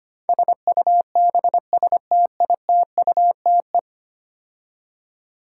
Morse code, heard as a keyed tone, about 25 wpm, 700 Hz. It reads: SUBSTITUTE
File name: 10 Letter Words - First 3 Letters Given - 25wpm